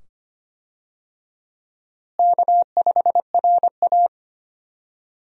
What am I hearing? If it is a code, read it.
Morse code, heard as a keyed tone, about 25 wpm, 700 Hz. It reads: K5RA
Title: USA Call Signs - 4 Characters - Repeated 2x - 25wpm